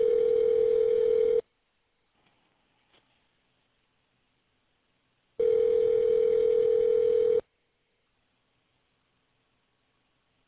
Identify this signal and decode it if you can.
Ringback tone (North America), 440 + 480 Hz.